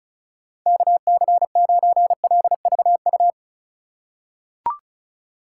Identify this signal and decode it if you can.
Morse code, heard as a keyed tone, about 35 wpm, 700 Hz. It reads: KC9LVU
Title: USA Call Signs - 4x Character Spacing - 35wpm